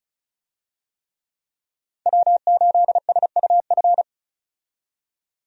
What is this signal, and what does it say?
Morse code, heard as a keyed tone, about 36 wpm, 700 Hz. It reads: W8SUF
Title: USA Call Signs - 5 Characters - 35wpm